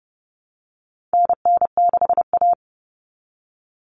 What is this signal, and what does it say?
Morse code, heard as a keyed tone, about 30 wpm, 700 Hz. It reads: NN6A
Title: Common Cut Numbers - 4-Digit 30wpm